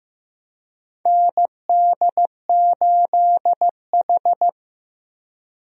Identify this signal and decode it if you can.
Morse code, heard as a keyed tone, about 15 wpm, 700 Hz. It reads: ND8H
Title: USA Call Signs - 4 Characters - Repeated 3x - 15wpm